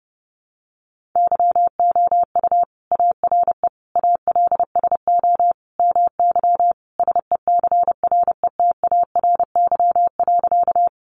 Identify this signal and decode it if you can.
Morse code, heard as a keyed tone, about 30 wpm, 700 Hz. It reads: YOU ARE ALSO MY SECRETARY.
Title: Sentences from Top 1300 Words 30wpm